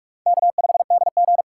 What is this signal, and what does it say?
Morse code, heard as a keyed tone, about 46 wpm, 700 Hz. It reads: K5DG
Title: USA Call Signs - 4 Characters - 4x Character Spacing - 45wpm